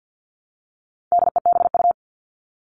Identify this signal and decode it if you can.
Morse code, heard as a keyed tone, about 50 wpm, 700 Hz. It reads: DEBU
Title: All Cut Numbers - 4-Digit 50wpm